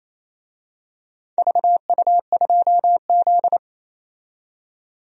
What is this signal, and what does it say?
Morse code, heard as a keyed tone, about 28 wpm, 700 Hz. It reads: VU2Z
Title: DXCC-100 Call Signs - 4 Characters - 28wpm